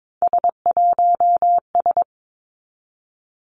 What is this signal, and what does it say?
Morse code, heard as a keyed tone, about 22 wpm, 700 Hz. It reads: S1S
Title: Three Challenging Letter-Number Combinations 22wpm